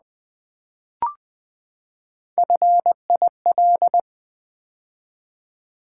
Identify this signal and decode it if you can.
Morse code, heard as a keyed tone, about 20 wpm, 700 Hz. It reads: FIL
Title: Build A Word - 3 Letters At A Time - 6-Letter Words - 20wpm